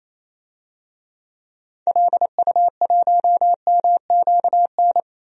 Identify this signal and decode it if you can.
Morse code, heard as a keyed tone, about 28 wpm, 700 Hz. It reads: LU1MQN
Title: DXCC-100 Call Signs - 6 Characters - 28wpm